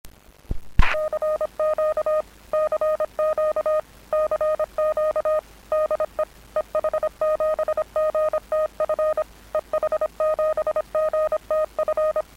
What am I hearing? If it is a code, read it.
Morse code, heard as a keyed tone, about 25 wpm, 620 Hz. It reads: CQ CQ CQ DE EH7GTF EH7GTF